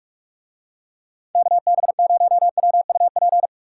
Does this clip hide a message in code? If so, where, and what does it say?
Yes: Morse code from 1.4 to 3.5 s, heard as a keyed tone, about 45 wpm, 700 Hz. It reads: KB0WUP